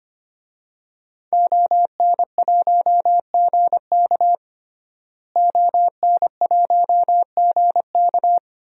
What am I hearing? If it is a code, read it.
Morse code, heard as a keyed tone, about 25 wpm, 700 Hz. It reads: ON1GK ON1GK
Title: DXCC-100 Call Signs - 5 Characters - Repeated 2x - 25wpm